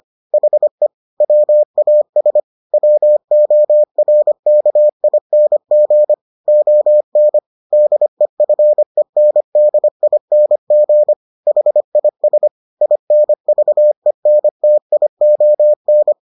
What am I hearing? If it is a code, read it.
Morse code, heard as a keyed tone, about 25 wpm, 600 Hz. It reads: HE WAS WORKING ON DEFENDING HIS INVENTION